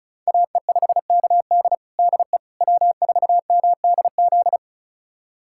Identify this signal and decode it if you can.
Morse code, heard as a keyed tone, about 36 wpm, 700 Hz. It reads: AE5KD DE W4MDZ
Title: USA Callsign DE Callsign - 5 Characters - 35wpm